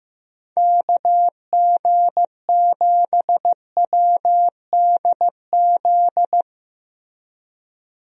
Morse code, heard as a keyed tone, about 15 wpm, 700 Hz. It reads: KG7WDZ
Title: USA Call Signs - 6 Characters - 15wpm